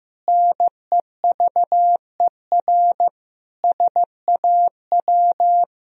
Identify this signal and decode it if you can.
Morse code, heard as a keyed tone, about 16 wpm, 700 Hz. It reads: NEVER SAW
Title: Sentences from Top 1000 Words - Review Part 3 - 15wpm